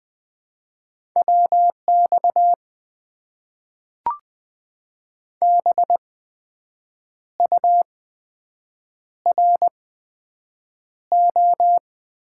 Morse code, heard as a keyed tone, about 20 wpm, spaced as for 7 wpm, 700 Hz. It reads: WX BURO
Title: Top 90 QSO Elements - 8x Character Spacing - 20wpm